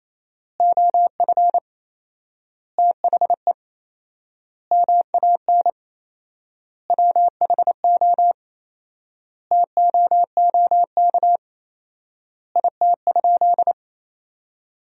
Morse code, heard as a keyed tone, about 27 wpm, 700 Hz. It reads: OF THE MAN WHO TOOK IT?
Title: 4x Word-Spacing - Sentences from Top 300 Words - 28wpm